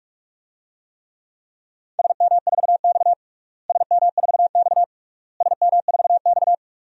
Morse code, heard as a keyed tone, about 44 wpm, 700 Hz. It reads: SM4X SM4X SM4X
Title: DXCC-100 Call Signs - 4 Characters - Repeated 3x - 45wpm